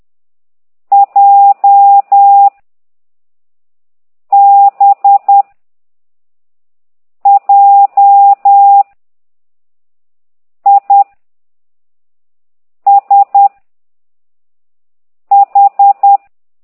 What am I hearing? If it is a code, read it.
Morse code, heard as a keyed tone, about 10 wpm, 800 Hz. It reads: J B J I S H
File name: ct2idw / exerc6